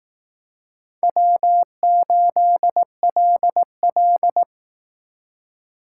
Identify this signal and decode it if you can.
Morse code, heard as a keyed tone, about 19 wpm, 700 Hz. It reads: W8LL